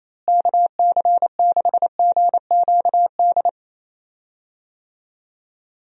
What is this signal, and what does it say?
Morse code, heard as a keyed tone, about 28 wpm, 700 Hz. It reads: KC6GQD
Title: USA Call Signs - 6 Characters - Repeated 2x - 28wpm